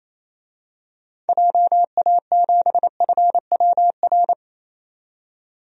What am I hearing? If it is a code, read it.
Morse code, heard as a keyed tone, about 28 wpm, 700 Hz. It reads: JA7FWR
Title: DXCC-100 Call Signs - 6 Characters - 28wpm